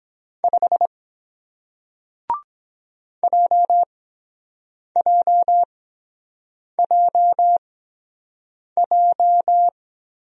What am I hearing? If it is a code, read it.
Morse code, heard as a keyed tone, about 23 wpm, spaced as for 7 wpm, 700 Hz. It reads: 5 JJJJ